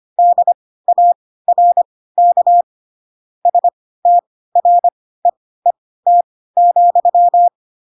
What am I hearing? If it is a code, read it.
Morse code, heard as a keyed tone, about 25 wpm, spaced as for 16 wpm, 700 Hz. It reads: DARK STREET,